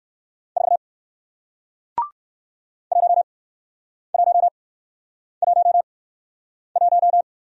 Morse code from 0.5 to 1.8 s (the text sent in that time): 4